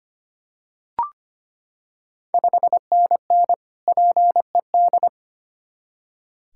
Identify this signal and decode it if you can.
Morse code, heard as a keyed tone, about 25 wpm, 700 Hz. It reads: 5NN PED
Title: Ontario QSO Party Exchange 25wpm